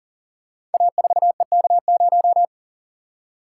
Morse code, heard as a keyed tone, about 41 wpm, 700 Hz. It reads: A4EK0
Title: Code Groups - 5 Letter:Numbers - 40wpm